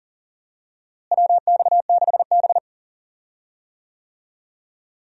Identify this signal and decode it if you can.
Morse code, heard as a keyed tone, about 40 wpm, 700 Hz. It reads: WX6B
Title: USA Call Signs - 4 Characters - Repeated 2x - 40wpm